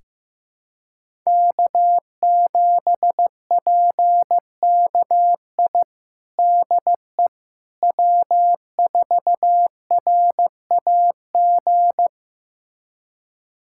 Morse code, heard as a keyed tone, about 15 wpm, 700 Hz. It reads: K7PKI DE W4RAG
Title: USA Callsign DE Callsign - 5 Characters - 15wpm